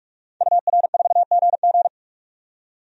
Morse code, heard as a keyed tone, about 42 wpm, 700 Hz. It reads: UR4GG
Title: DXCC-100 Call Signs - 5 Characters - 45wpm